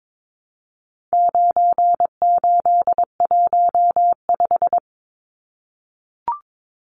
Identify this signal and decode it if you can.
Morse code, heard as a keyed tone, about 22 wpm, 700 Hz. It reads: 9815
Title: Numbers - 4-Digit 22wpm